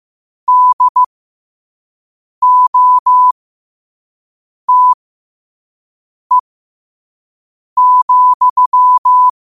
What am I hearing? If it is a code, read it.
Morse code, heard as a keyed tone, about 15 wpm, spaced as for 5 wpm, 1000 Hz. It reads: DOTE,
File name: qotdfiles / 5wpm